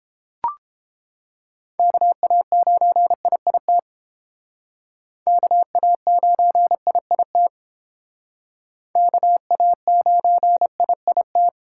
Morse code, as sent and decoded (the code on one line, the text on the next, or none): -.- .- ----. .. .. - / -.- .- ----. .. .. - / -.- .- ----. .. .. -
KA9IIT KA9IIT KA9IIT